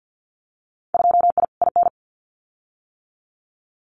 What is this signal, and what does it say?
Morse code, heard as a keyed tone, about 50 wpm, 700 Hz. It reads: 2I IN
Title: Winter Field Day Exchanges Encore 50wpm